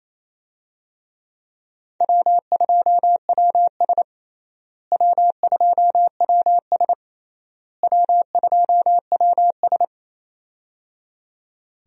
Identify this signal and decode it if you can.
Morse code, heard as a keyed tone, about 28 wpm, 700 Hz. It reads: W2WS W2WS W2WS
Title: USA Call Signs - 4 Characters - Repeated 3x - 28wpm